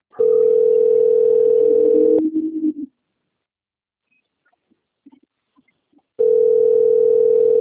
Ringback tone (North America), 440 + 480 Hz.